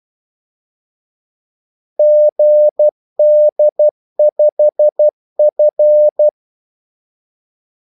Morse code, heard as a keyed tone, about 12 wpm, 600 Hz. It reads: GD5F
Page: 2 ekstra ordmellemrum mellem kaldesignalerne